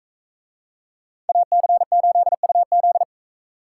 Morse code, heard as a keyed tone, about 40 wpm, 700 Hz. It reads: AC8UZ